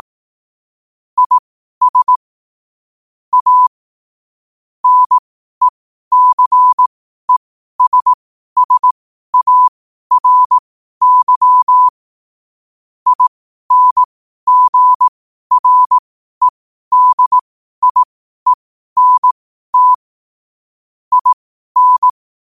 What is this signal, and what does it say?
Morse code, heard as a keyed tone, about 18 wpm, spaced as for 12 wpm, 1000 Hz. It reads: IS A NECESSARY INGREDIENT IN
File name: qotdfiles / 12wpm